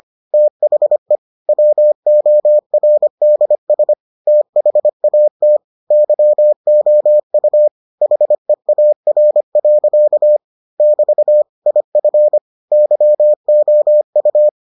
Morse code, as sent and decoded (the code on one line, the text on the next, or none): - .... / .-- --- .-. -.. ... / - .... .- - / -.-- --- ..- / .... .- .-. .-.-.- / -...- / .. ..-. / -.-- --- ..-
THE WORDS THAT YOU HEAR. <BT> IF YOU